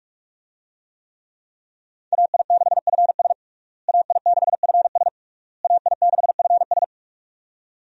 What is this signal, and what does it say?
Morse code, heard as a keyed tone, about 46 wpm, 700 Hz. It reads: AI6FS AI6FS AI6FS
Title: USA Call Signs - 5 Characters - Repeated 3x - 45wpm